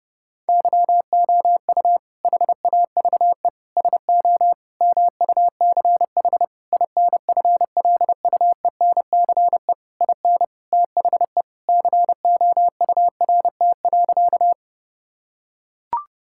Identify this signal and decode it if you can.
Morse code, heard as a keyed tone, about 32 wpm, 700 Hz. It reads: YOU HAVE SO MUCH INFLUENCE IN THE COURT.
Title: Sentences from Top 1600 Words 30wpm